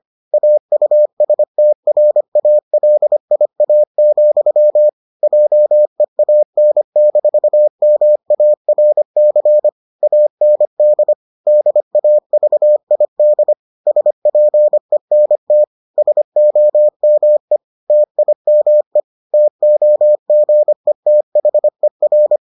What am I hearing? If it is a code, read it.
Morse code, heard as a keyed tone, about 25 wpm, 600 Hz. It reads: AUSTRALIA, JEAN-MARC AND DAVID SPENT SOME TIME TOGETHER